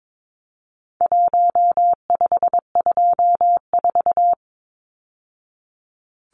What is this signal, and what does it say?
Morse code, heard as a keyed tone, about 22 wpm, 700 Hz. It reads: 1524
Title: Numbers - 4-Digit 22wpm